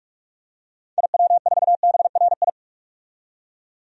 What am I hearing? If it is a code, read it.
Morse code, heard as a keyed tone, about 46 wpm, 700 Hz. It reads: IW4BRI
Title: DXCC-100 Call Signs - 6 Characters - 45wpm